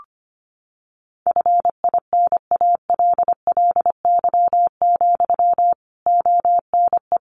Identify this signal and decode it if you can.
Morse code, heard as a keyed tone, about 25 wpm, 700 Hz. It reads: FINALLY, ONE
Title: Sentences from Top 1300 Words 25wpm